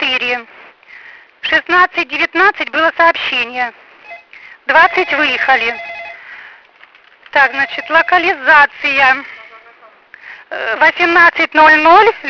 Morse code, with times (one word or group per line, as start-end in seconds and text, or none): none